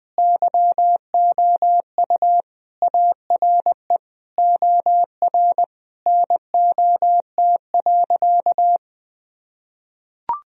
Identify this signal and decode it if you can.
Morse code, heard as a keyed tone, about 21 wpm, 700 Hz. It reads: YOU ARE OR NOT.
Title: Sentences from Top 1000 Words - Review Part 3 - 20wpm